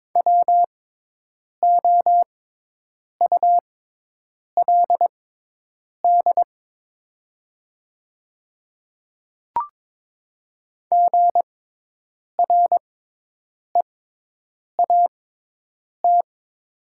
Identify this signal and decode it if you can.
Morse code, heard as a keyed tone, about 22 wpm, spaced as for 8 wpm, 700 Hz. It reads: WOULD GREAT
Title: 6x Character Spacing - Top 200 Words - 22wpm